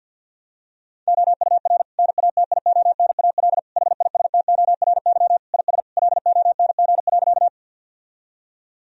Morse code, heard as a keyed tone, about 50 wpm, 700 Hz. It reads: OUR NATIONAL HISTORY IS LONG.
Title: Sentences from Top 1000 Words - Review Part 4 - 50wpm